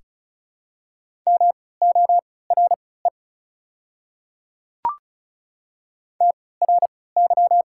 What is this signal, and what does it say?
Morse code, heard as a keyed tone, about 36 wpm, spaced as for 20 wpm, 700 Hz. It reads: MORE TRY
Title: 3x Character Spacing - Top 200 Words - 35wpm